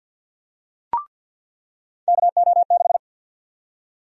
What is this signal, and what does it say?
Morse code, heard as a keyed tone, about 49 wpm, 700 Hz. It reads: KO6
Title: USA Call Sign Prefixes - 50wpm